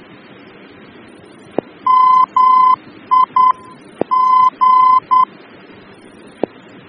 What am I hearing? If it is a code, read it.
Morse code, heard as a keyed tone, about 10 wpm, 1020 Hz. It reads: MI G
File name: germany / mx / MAG-D-VOR